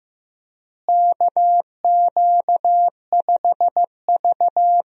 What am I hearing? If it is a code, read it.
Morse code, heard as a keyed tone, about 15 wpm, 700 Hz. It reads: KQ5V
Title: USA Call Signs - 4 Characters - Repeated 2x - 15wpm